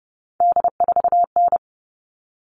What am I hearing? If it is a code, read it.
Morse code, heard as a keyed tone, about 28 wpm, 700 Hz. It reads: D4N
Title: All Cut Numbers - 3-Digit 30wpm